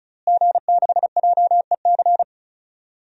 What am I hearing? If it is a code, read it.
Morse code, heard as a keyed tone, about 35 wpm, 700 Hz. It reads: G6JEC
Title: DXCC-100 Call Signs - 5 Characters - Repeated 2x - 35wpm